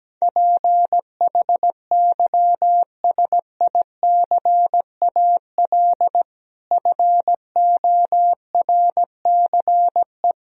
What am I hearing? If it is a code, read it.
Morse code, heard as a keyed tone, about 17 wpm, 700 Hz. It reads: PHYSICAL FORCE
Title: Sentences from Top 1000 Words - Review Part 4 - 17wpm